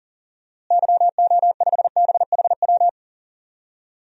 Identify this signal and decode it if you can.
Morse code, heard as a keyed tone, about 40 wpm, 700 Hz. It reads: YO5BHW
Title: DXCC-100 Call Signs - 6 Characters - Repeated 2x - 40wpm